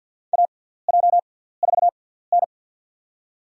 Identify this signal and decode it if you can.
Morse code, heard as a keyed tone, about 50 wpm, spaced as for 17 wpm, 700 Hz. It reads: AJ4N
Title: USA Call Signs - 6x Character Spacing - 50wpm